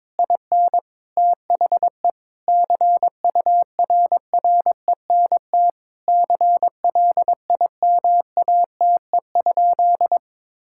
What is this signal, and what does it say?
Morse code, heard as a keyed tone, about 22 wpm, 700 Hz. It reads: IN THE CURRENT CLIMATE?
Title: Sentences from Top 1600 Words 22wpm